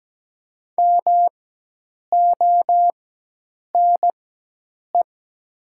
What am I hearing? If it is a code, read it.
Morse code, heard as a keyed tone, about 17 wpm, spaced as for 8 wpm, 700 Hz. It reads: MONE
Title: 4x Character Spacing - Top 500 Words - 17wpm